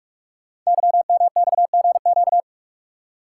Morse code, heard as a keyed tone, about 44 wpm, 700 Hz. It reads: YMXGQ
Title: Code Groups - 5 Letters - 45wpm